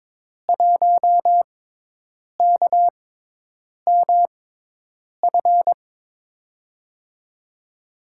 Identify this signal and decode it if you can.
Morse code, heard as a keyed tone, about 21 wpm, spaced as for 8 wpm, 700 Hz. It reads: 1KMF
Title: USA Call Signs - 5 Characters - 6x Character Spacing - 22wpm